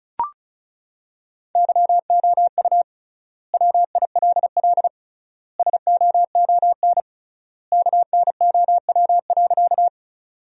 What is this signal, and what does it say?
Morse code, heard as a keyed tone, about 36 wpm, 700 Hz. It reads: YOU WILL SOON KNOW.
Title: 3x Word-Spacing - Sentences from Top 400 Words - 35wpm